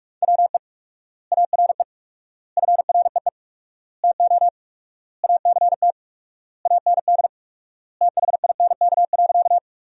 Morse code, heard as a keyed tone, about 41 wpm, 700 Hz. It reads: WE ARE FREE TO ACT AND THINK.
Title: 4x Word-Spacing - Sentences from Top 500 Words - 45wpm